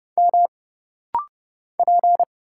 Morse code, heard as a keyed tone, about 30 wpm, 700 Hz. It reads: M P
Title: Single Letters - Rapid-Fire 30wpm